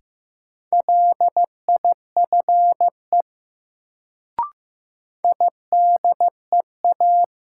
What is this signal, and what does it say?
Morse code, heard as a keyed tone, about 15 wpm, 700 Hz. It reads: LIFE IDEA
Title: Sentences from Top 300 Words - New Word Drill - 15wpm